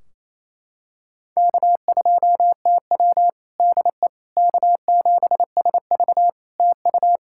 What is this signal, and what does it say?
Morse code, heard as a keyed tone, about 28 wpm, 700 Hz. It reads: K2TW DE K7SV TU
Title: K1USN SST QSO Practice 28wpm